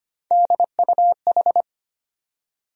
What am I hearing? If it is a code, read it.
Morse code, heard as a keyed tone, about 25 wpm, 700 Hz. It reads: DUH
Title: USA Call Sign Postfixes - 3 Letters 25wpm